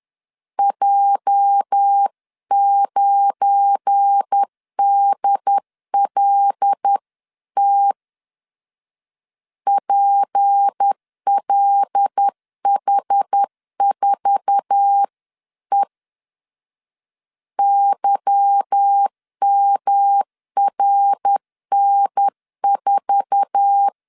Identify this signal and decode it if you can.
Morse code, heard as a keyed tone, about 10 wpm, 800 Hz. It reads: J9DLT PLH4E YMRN4
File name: LEZIONI / 8 YJ45 / 8 YJ4 10wpm 2633